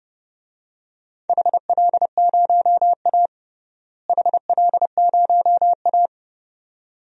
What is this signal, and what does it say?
Morse code, heard as a keyed tone, about 30 wpm, 700 Hz. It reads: HL0A HL0A
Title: DXCC-100 Call Signs - 4 Characters - Repeated 2x - 30wpm